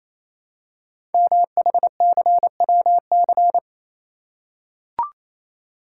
Morse code, heard as a keyed tone, about 28 wpm, 700 Hz. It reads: MHCWC
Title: Code Groups - 5 Letter:Numbers - 28wpm